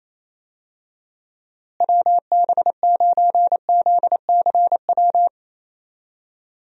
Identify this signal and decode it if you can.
Morse code, heard as a keyed tone, about 28 wpm, 700 Hz. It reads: WB9ZCW
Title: USA Call Signs - 6 Characters - Repeated 2x - 28wpm